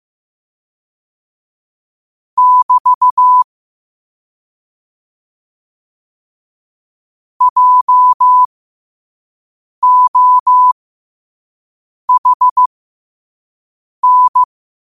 Morse code, heard as a keyed tone, about 15 wpm, spaced as for 5 wpm, 1000 Hz. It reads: <BT> JOHN
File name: qotdfiles / 5wpm